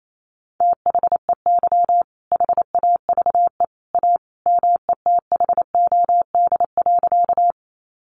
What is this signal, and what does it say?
Morse code, heard as a keyed tone, about 28 wpm, 700 Hz. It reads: THEY HAVE A METHOD.
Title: Sentences from Top 600 Words 28wpm